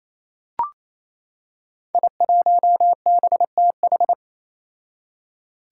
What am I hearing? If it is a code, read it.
Morse code, heard as a keyed tone, about 28 wpm, 700 Hz. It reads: I1BTH
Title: Five Challenging Letter-Number Combinations 28wpm